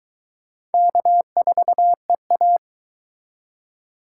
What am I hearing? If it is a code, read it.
Morse code, heard as a keyed tone, about 22 wpm, 700 Hz. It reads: K4EA